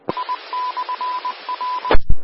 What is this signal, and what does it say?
Morse code, heard as a keyed tone, about 20 wpm, 1000 Hz. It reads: I/R
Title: Austin, TX Amateur Repeaters